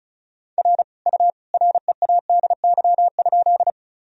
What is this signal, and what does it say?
Morse code, heard as a keyed tone, about 34 wpm, 700 Hz. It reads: R U READY?
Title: Sentences from Top 400 Words and Abbreviations 35wpm